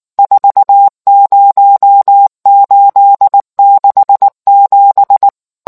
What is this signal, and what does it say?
Morse code, heard as a keyed tone, about 19 wpm, 800 Hz. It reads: 40867